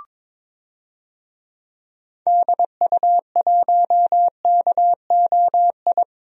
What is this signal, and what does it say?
Morse code, heard as a keyed tone, about 22 wpm, 700 Hz. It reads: DU1KOI
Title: DXCC-100 Call Signs - 6 Characters - 22wpm